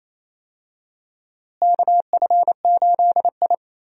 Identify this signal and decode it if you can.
Morse code, heard as a keyed tone, about 28 wpm, 700 Hz. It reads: KF8I
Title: USA Call Signs - 4 Characters - Repeated 2x - 28wpm